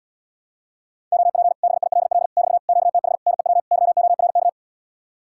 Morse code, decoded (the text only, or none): OP BILL <BT> QTH NEW YORK